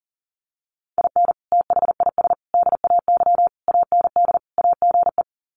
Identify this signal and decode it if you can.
Morse code, heard as a keyed tone, about 40 wpm, 700 Hz. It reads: IN THIS DAY AND AGE
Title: Sets of 5 Words Encore 40wpm